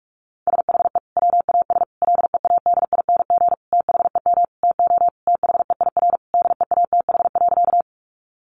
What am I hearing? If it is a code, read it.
Morse code, heard as a keyed tone, about 44 wpm, 700 Hz. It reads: SHE WAS LEADING THEM TO THEIR DEATH.
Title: Sentences from Top 1100 Words 45wpm